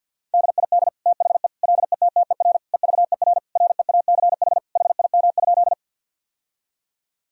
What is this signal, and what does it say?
Morse code, heard as a keyed tone, about 50 wpm, 700 Hz. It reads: DID THE LETTER EVER REACH HIM?
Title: Sentences from Top 1000 Words - Review Part 2 - 50wpm